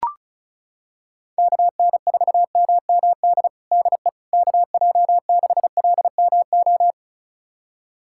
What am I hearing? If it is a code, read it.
Morse code, heard as a keyed tone, about 36 wpm, 700 Hz. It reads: KN4MMD DE KJ6LMO
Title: USA Callsign DE Callsign - 6 Characters - 35wpm